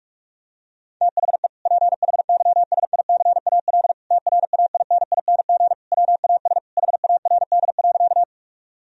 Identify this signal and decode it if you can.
Morse code, heard as a keyed tone, about 44 wpm, 700 Hz. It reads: THE PHYSICAL TRAINING WAS HARD.
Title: Sentences from Top 1000 Words - Review Part 4 - 45wpm